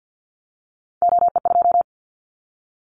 Morse code, heard as a keyed tone, about 50 wpm, 700 Hz. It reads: OE2
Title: Callsign Prefixes for Top 50 DXCC - 50wpm